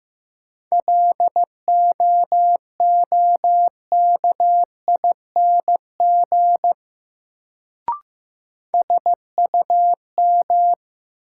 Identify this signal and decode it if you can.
Morse code, heard as a keyed tone, about 16 wpm, 700 Hz. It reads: LOOKING SUM
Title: Sentences from Top 1000 Words - Review Part 6 - New Word Drill - 15wpm